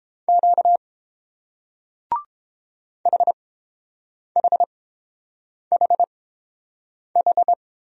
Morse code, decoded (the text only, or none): Q HHHH